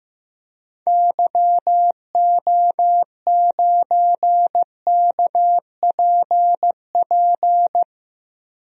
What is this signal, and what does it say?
Morse code, heard as a keyed tone, about 15 wpm, 700 Hz. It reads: YO9KPP
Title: DXCC-100 Call Signs - 6 Characters - Repeated 2x - 15wpm